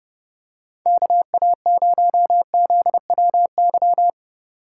Morse code, heard as a keyed tone, about 31 wpm, 700 Hz. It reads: KA0ZWY